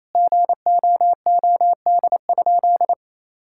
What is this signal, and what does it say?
Morse code, heard as a keyed tone, about 28 wpm, 700 Hz. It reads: GOOD?
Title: Sentences from Top 1600 Words 28wpm